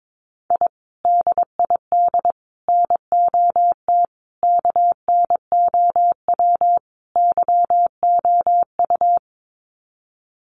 Morse code, heard as a keyed tone, about 22 wpm, 700 Hz. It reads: I DID NOT KNOW YOU
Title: Sets of 5 Words Encore2 22wpm